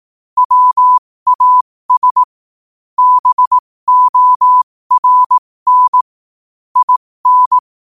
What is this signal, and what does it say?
Morse code, heard as a keyed tone, about 18 wpm, spaced as for 15 wpm, 1000 Hz. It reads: WAS BORN IN
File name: qotdfiles / 15wpm